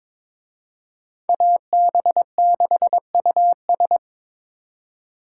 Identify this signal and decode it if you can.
Morse code, heard as a keyed tone, about 26 wpm, 700 Hz. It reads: AB6US
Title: Speed Racing - USA 2x2 Call Signs 22wpm